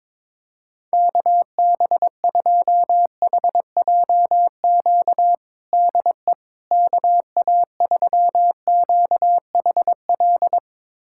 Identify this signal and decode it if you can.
Morse code, heard as a keyed tone, about 22 wpm, 700 Hz. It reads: KB2HJQ DE KA3QHL
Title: USA Callsign DE Callsign 22wpm